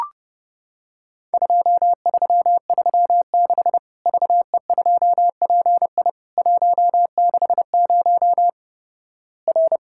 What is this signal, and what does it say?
Morse code, heard as a keyed tone, about 30 wpm, 700 Hz. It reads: 2336 VE2PI 160 R
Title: Worked All Europe DX Contest - QTC Exchange - 30wpm